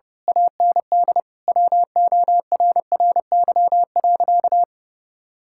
Morse code, heard as a keyed tone, about 32 wpm, 700 Hz. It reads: AND WORRY.
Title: Pangram Challenge v2 - 30wpm